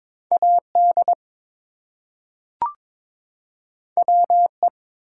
Morse code, heard as a keyed tone, about 22 wpm, 700 Hz. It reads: AD WE
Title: Two Letter Words - 22wpm